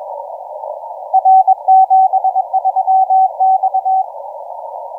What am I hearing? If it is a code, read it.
Morse code, heard as a keyed tone, about 22 wpm, 760 Hz, with radio noise. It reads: R 73 X
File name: TO5PX 1949UTC 12m